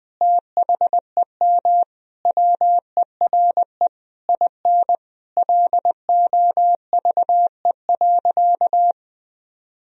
Morse code, heard as a keyed tone, about 20 wpm, 700 Hz. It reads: THEM WERE IN LOVE.
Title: Sentences from Top 1000 Words - Review Part 4 - 20wpm